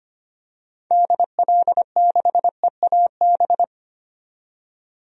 Morse code, heard as a keyed tone, about 25 wpm, 700 Hz. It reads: DL6EAB
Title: DXCC-100 Call Signs - 6 Characters - 25wpm